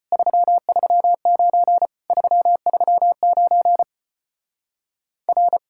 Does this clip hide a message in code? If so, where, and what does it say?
Yes: Morse code from 0.1 to 5.6 s, heard as a keyed tone, about 33 wpm, 700 Hz. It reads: 339 339 R